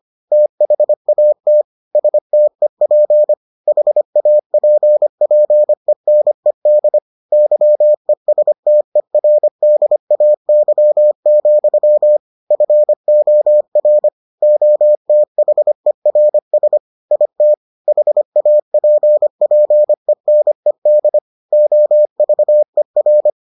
Morse code, heard as a keyed tone, about 25 wpm, 600 Hz. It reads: THAT STEP HAPPENED YESTERDAY, FOR OTHERS IT HAPPENED OVER